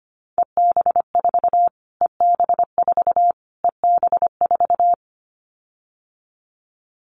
Morse code, heard as a keyed tone, about 25 wpm, 700 Hz. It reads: EB4 EB4 EB4
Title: Callsign Prefixes for Top 25 DXCC - Repeated 3x - 25wpm